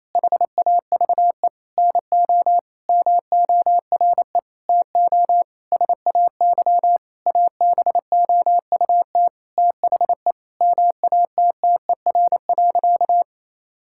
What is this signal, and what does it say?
Morse code, heard as a keyed tone, about 28 wpm, 700 Hz. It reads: HAVE NO MORE TO SAY ABOUT THE MATTER.
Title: Sentences from Top 1000 Words - Review Part 5 - 28wpm